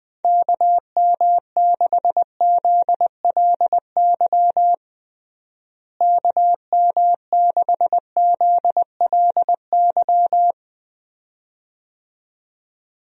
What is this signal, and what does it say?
Morse code, heard as a keyed tone, about 21 wpm, 700 Hz. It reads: KM6ZLY KM6ZLY
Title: USA Call Signs - 6 Characters - Repeated 2x - 20wpm